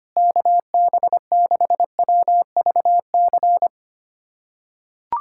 Morse code, heard as a keyed tone, about 25 wpm, 700 Hz. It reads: KB6WVC
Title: USA Callsign DE Callsign - 6 Characters - 25wpm